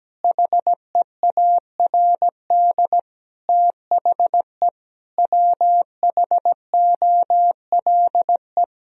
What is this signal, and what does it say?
Morse code, heard as a keyed tone, about 17 wpm, 700 Hz. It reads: HEARD THE WHOLE
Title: Sentences from Top 1000 Words - Review Part 3 - 17wpm